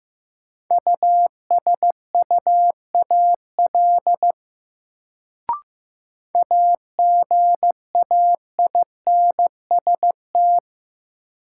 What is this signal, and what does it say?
Morse code, heard as a keyed tone, about 15 wpm, 700 Hz. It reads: USUAL AGAINST
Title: Sentences from Top 400 Words - New Word Drill - 15wpm